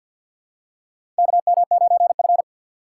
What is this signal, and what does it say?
Morse code, heard as a keyed tone, about 52 wpm, 700 Hz. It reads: KM9F
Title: USA Call Signs - 4 Characters - 3x Character Spacing - 50wpm